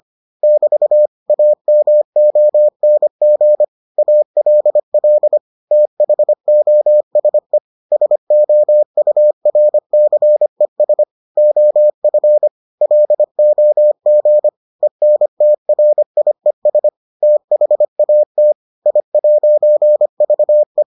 Morse code, heard as a keyed tone, about 25 wpm, 600 Hz. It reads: <BT> AMONG ALL THOSE SOURCES OF LOG ENTRIES THAT I'VE